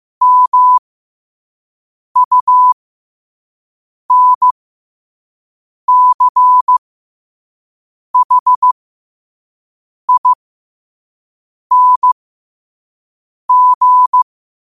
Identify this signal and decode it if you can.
Morse code, heard as a keyed tone, about 15 wpm, spaced as for 5 wpm, 1000 Hz. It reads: MUNCHING
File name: qotdfiles / 5wpm